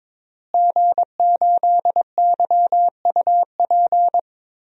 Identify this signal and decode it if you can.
Morse code, heard as a keyed tone, about 21 wpm, 700 Hz. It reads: G8YUP
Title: DXCC-100 Call Signs - 5 Characters - Repeated 2x - 22wpm